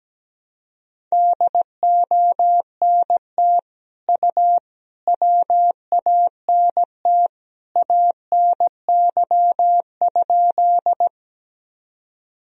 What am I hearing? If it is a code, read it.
Morse code, heard as a keyed tone, about 17 wpm, 700 Hz. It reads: DONT U WANT ANY?
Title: Sentences from Top 300 Words and Abbreviations 17wpm